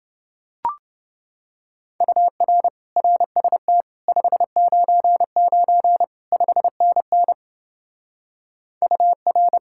Morse code, heard as a keyed tone, about 29 wpm, 700 Hz. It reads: UR RST 599 5NN UR